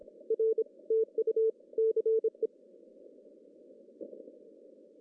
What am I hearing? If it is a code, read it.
Morse code, heard as a keyed tone, about 26 wpm, 440 Hz. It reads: R TU CE
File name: CW / cqww-cw-k5tr